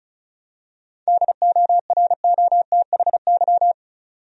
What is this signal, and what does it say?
Morse code, heard as a keyed tone, about 36 wpm, 700 Hz. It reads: DOROTHY
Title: USA Common Names - 35wpm